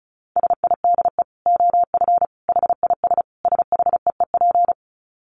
Morse code, heard as a keyed tone, about 34 wpm, 700 Hz. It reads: SIDE OF HIS SHEEP
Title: Dit Challenge 35wpm